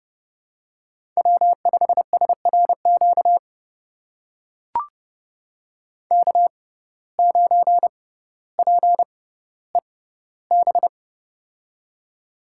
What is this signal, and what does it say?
Morse code, heard as a keyed tone, about 29 wpm, spaced as for 15 wpm, 700 Hz. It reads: W5SRQ K9PEB
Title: USA Call Signs - 5 Characters - 6x Character Spacing - 30wpm